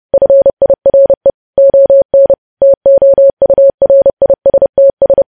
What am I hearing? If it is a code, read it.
Morse code, heard as a keyed tone, about 30 wpm, 550 Hz. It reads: FIRE ON TOURISTS
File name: podcast / News headlines-1x-30x30